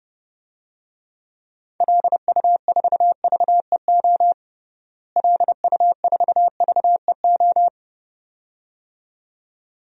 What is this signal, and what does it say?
Morse code, heard as a keyed tone, about 32 wpm, 700 Hz. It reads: LU4VEO LU4VEO
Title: DXCC-100 Call Signs - 6 Characters - Repeated 2x - 30wpm